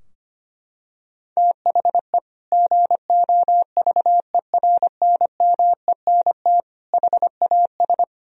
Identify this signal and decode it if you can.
Morse code, heard as a keyed tone, about 25 wpm, 700 Hz. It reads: THE GOVERNMENT HAS
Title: Sentences from Top 1000 Words - Review Part 3 - 25wpm